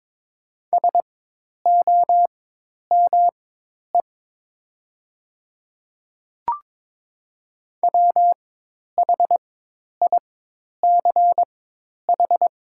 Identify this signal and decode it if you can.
Morse code, heard as a keyed tone, about 22 wpm, spaced as for 10 wpm, 700 Hz. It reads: SOME WHICH
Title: 4x Character Spacing - Top 500 Words - 22wpm